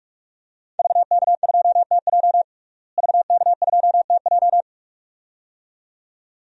Morse code, heard as a keyed tone, about 44 wpm, 700 Hz. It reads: VK2TJ VK2TJ
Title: DXCC-100 Call Signs - 5 Characters - Repeated 2x - 45wpm